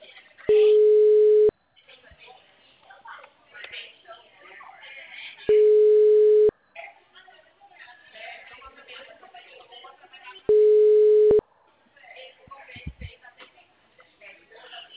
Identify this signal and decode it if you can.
Ringback tone (Europe, 425 Hz), with speech.